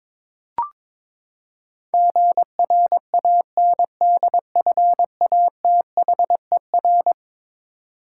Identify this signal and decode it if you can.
Morse code, heard as a keyed tone, about 22 wpm, 700 Hz. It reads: GRANDFATHER
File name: practice / 22wpm / Eleven Letter Words - 22wpm